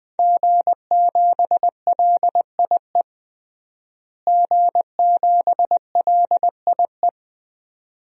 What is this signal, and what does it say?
Morse code, heard as a keyed tone, about 21 wpm, 700 Hz. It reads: G7LIE G7LIE
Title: DXCC-100 Call Signs - 5 Characters - Repeated 2x - 20wpm